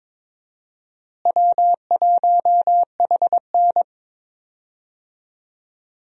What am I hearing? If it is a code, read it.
Morse code, heard as a keyed tone, about 22 wpm, 700 Hz. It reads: W1HN
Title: USA Call Signs - 4 Characters - Repeated 3x - 22wpm